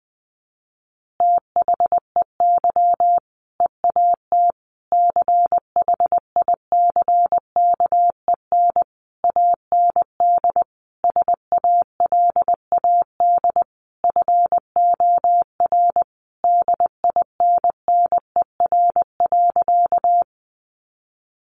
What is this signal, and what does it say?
Morse code, heard as a keyed tone, about 20 wpm, 700 Hz. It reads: THEY EAT CHICKEN AND SALAD FOR DINNER.
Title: Easy Phrases 20wpm